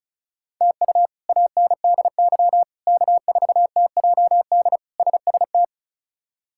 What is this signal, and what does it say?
Morse code, heard as a keyed tone, about 36 wpm, 700 Hz. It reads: TU ANDY K4TJD SST
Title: K1USN SST QSO Practice - 35wpm